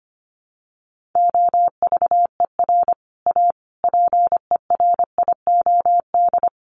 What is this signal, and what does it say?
Morse code, heard as a keyed tone, about 25 wpm, 700 Hz. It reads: OVER A PERIOD
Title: Sets of 5 Words Encore2 25wpm